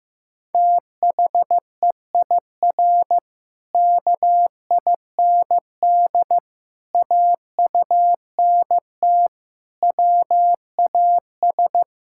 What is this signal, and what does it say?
Morse code, heard as a keyed tone, about 15 wpm, 700 Hz. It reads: THEIR KIND AUNT WAS
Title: Pangram Challenge v2 - 15wpm